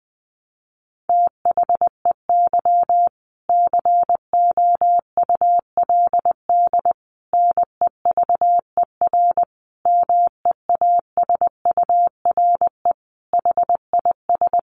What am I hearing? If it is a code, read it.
Morse code, heard as a keyed tone, about 20 wpm, 700 Hz. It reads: THEY COULD NEVER MEASURE HIS
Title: Sentences from Top 1200 Words 20wpm